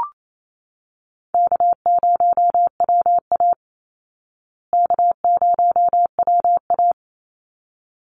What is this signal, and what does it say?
Morse code, heard as a keyed tone, about 28 wpm, 700 Hz. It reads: K0WA K0WA
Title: Call Signs Repeated 3x - 28wpm